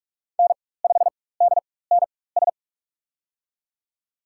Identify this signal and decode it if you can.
Morse code, heard as a keyed tone, about 46 wpm, spaced as for 21 wpm, 700 Hz. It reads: N5DNS
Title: USA Call Signs - 5 Characters - 4x Character Spacing - 45wpm